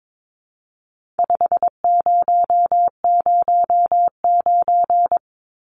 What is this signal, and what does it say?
Morse code, heard as a keyed tone, about 22 wpm, 700 Hz. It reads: 5009
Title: Numbers - 4-Digit 22wpm